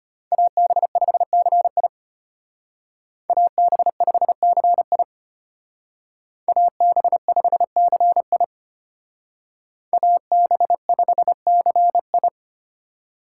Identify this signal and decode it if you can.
Morse code, heard as a keyed tone, about 30 wpm, 700 Hz. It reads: AB5CI AB5CI AB5CI AB5CI
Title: Speed Racing - USA 2x2 Call Signs 25wpm